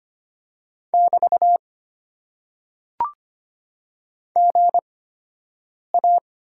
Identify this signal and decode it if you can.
Morse code, heard as a keyed tone, about 25 wpm, spaced as for 7 wpm, 700 Hz. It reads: <BT> GA
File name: Top 90 QSO Elements - 8x Character Spacing - 25wpm